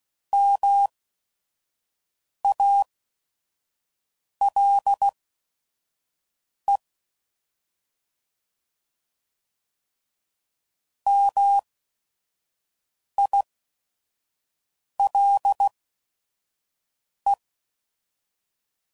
Morse code, heard as a keyed tone, about 16 wpm, spaced as for 5 wpm, 800 Hz. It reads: MALE MILE